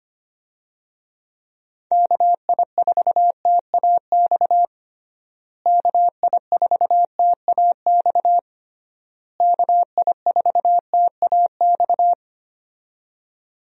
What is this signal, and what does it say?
Morse code, heard as a keyed tone, about 25 wpm, 700 Hz. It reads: KI4TAX KI4TAX KI4TAX
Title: USA Call Signs - 6 Characters - Repeated 3x - 25wpm